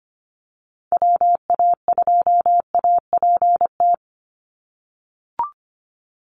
Morse code, heard as a keyed tone, about 25 wpm, 700 Hz. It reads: WA2APT
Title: USA Call Signs 25wpm